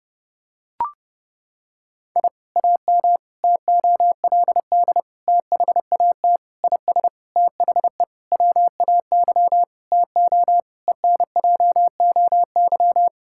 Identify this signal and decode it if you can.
Morse code, heard as a keyed tone, about 30 wpm, 700 Hz. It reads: I AM TOLD THAT IS THE WAY TO ENJOY
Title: Sentences from Top 1000 Words - Review Part 5 - 30wpm